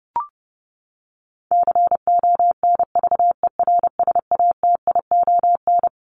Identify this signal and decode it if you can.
Morse code, heard as a keyed tone, about 30 wpm, 700 Hz. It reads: CONVERSATION
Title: Top 1751-2000 Words 30wpm